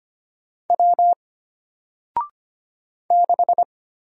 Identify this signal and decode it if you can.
Morse code, heard as a keyed tone, about 25 wpm, 700 Hz. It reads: W 6
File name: Single-Letter-Number-25wpm